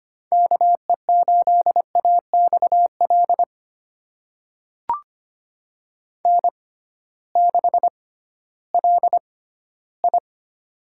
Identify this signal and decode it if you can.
Morse code, heard as a keyed tone, about 25 wpm, spaced as for 15 wpm, 700 Hz. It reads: KE8AXL N6LI
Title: USA Call Signs - 6x Character Spacing - 25wpm